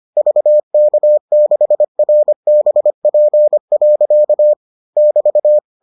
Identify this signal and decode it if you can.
Morse code, heard as a keyed tone, about 25 wpm, 600 Hz. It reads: VK6RBP. <BT>